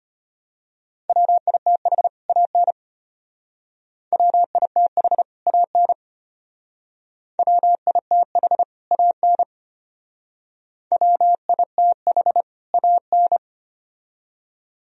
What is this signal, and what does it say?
Morse code, heard as a keyed tone, about 32 wpm, 700 Hz. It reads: WITH AN WITH AN WITH AN WITH AN